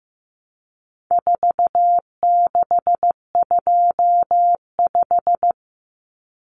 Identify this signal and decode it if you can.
Morse code, heard as a keyed tone, about 15 wpm, 700 Hz. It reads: 4625
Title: Common Cut Numbers - 4-Digit 15wpm